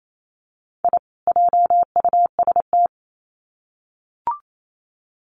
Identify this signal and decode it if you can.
Morse code, heard as a keyed tone, about 28 wpm, 700 Hz. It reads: I JUST
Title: Sets of 2 Words 28wpm